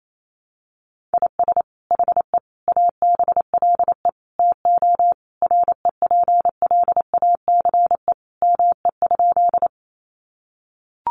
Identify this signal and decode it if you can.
Morse code, heard as a keyed tone, about 28 wpm, 700 Hz. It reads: IS HE ABLE TO REPLACE ME?
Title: Sentences from Top 1400 Words 28wpm